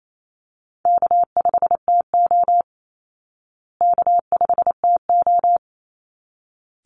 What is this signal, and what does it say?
Morse code, heard as a keyed tone, about 28 wpm, 700 Hz. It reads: K5TO K5TO
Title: Easy Call Signs Repeated 2x - 28wpm